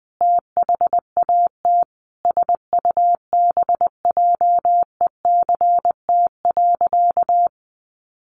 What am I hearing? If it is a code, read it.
Morse code, heard as a keyed tone, about 20 wpm, 700 Hz. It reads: THAT SUBJECT.
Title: Sentences from Top 800 Words 20wpm